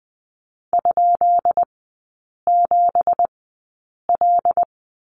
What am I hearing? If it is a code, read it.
Morse code, heard as a keyed tone, about 20 wpm, spaced as for 8 wpm, 700 Hz. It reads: ?7L
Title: 184 Lesson 30e - Letter b - Three Characters 20wpm